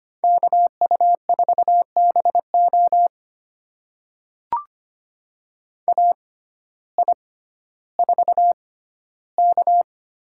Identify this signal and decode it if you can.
Morse code, heard as a keyed tone, about 25 wpm, spaced as for 14 wpm, 700 Hz. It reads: KU4BO AI4K
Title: USA Call Signs - 5 Characters - 6x Character Spacing - 25wpm